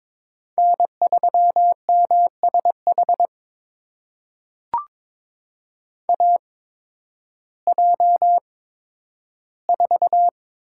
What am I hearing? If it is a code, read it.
Morse code, heard as a keyed tone, about 22 wpm, spaced as for 12 wpm, 700 Hz. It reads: N3MSH AJ4
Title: USA Call Signs - 5 Characters - 8x Character Spacing - 22wpm